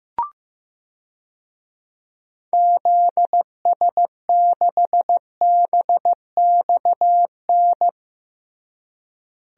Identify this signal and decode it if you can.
Morse code, heard as a keyed tone, about 15 wpm, 700 Hz. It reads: ZS6BXN
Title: DXCC-100 Call Signs - 6 Characters - Repeated 3x - 15wpm